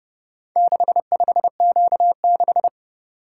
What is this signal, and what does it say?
Morse code, heard as a keyed tone, about 29 wpm, 700 Hz. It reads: 65Q6